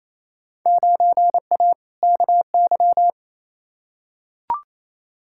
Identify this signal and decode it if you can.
Morse code, heard as a keyed tone, about 28 wpm, 700 Hz. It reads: 9A KY
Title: Field Day Exchanges Encore - 28wpm